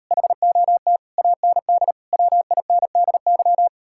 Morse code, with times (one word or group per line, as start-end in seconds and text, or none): 0.1-1.0 HOT
1.2-1.9 AND
2.1-3.7 WINDY